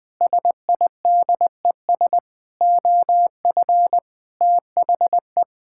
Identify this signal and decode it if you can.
Morse code, heard as a keyed tone, about 20 wpm, 700 Hz. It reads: SIDES OF THE
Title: Sentences with 6-Word Phrases - Spoken Beginning - 20wpm